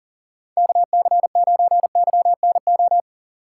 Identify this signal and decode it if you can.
Morse code, heard as a keyed tone, about 50 wpm, 700 Hz. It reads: KC9YNO
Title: Speed Racing - USA 2x3 Call Signs 40wpm